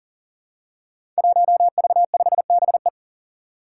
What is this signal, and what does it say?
Morse code, heard as a keyed tone, about 40 wpm, 700 Hz. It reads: 1V5BE
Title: Five Challenging Letter-Number Combinations 40wpm